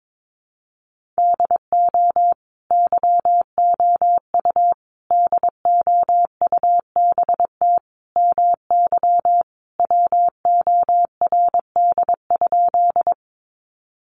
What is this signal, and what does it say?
Morse code, heard as a keyed tone, about 22 wpm, 700 Hz. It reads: DO YOU DOUBT MY WORD?
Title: Sentences from Top 1500 Words 22wpm